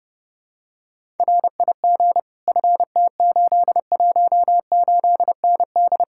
Morse code, heard as a keyed tone, about 30 wpm, 700 Hz. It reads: RIG FT818ND